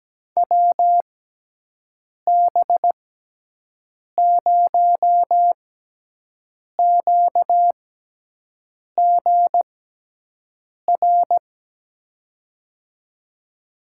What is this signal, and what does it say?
Morse code, heard as a keyed tone, about 17 wpm, spaced as for 6 wpm, 700 Hz. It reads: WB0QGR
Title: USA Call Signs - 6 Characters - 6x Character Spacing - 17wpm